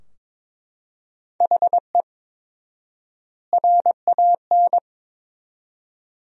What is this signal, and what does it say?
Morse code, heard as a keyed tone, about 22 wpm, 700 Hz. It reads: HE RAN
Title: 4x Word-Spacing - Sentences from Top 500 Words - 22wpm